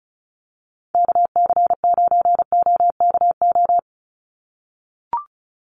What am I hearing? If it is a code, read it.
Morse code, heard as a keyed tone, about 34 wpm, 700 Hz. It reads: KC9OKO
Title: USA Call Signs 35wpm